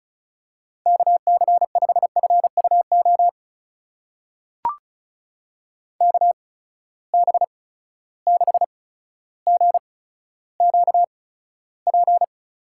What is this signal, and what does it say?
Morse code, heard as a keyed tone, about 36 wpm, spaced as for 15 wpm, 700 Hz. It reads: KC5FUO KB6GQP
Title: USA Call Signs - 6 Characters - 8x Character Spacing - 35wpm